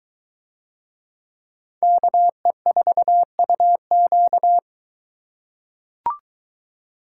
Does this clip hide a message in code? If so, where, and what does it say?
Yes: Morse code from 1.8 to 4.6 s, heard as a keyed tone, about 22 wpm, 700 Hz. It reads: KE4UQ